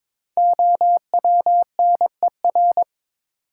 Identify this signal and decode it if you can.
Morse code, heard as a keyed tone, about 22 wpm, 700 Hz. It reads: OWNER
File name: 5 Letter Words - First 3 Letters Given - 22wpm